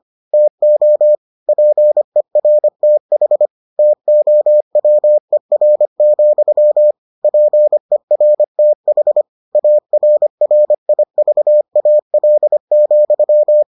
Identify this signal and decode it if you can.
Morse code, heard as a keyed tone, about 25 wpm, 600 Hz. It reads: TO PERTH TOWER, PERTH ARRIVAL,